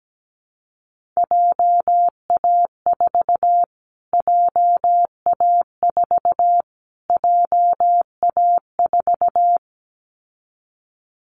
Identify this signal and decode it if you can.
Morse code, heard as a keyed tone, about 17 wpm, 700 Hz. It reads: JA4 JA4 JA4
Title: Callsign Prefixes for Top 25 DXCC - Repeated 3x - 17wpm